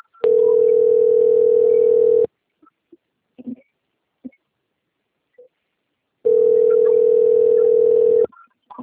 Ringback tone (North America), 440 + 480 Hz.